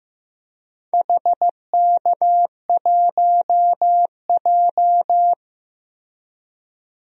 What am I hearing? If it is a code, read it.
Morse code, heard as a keyed tone, about 15 wpm, 700 Hz. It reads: HK1J
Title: DXCC-100 Call Signs - 4 Characters - Repeated 2x - 15wpm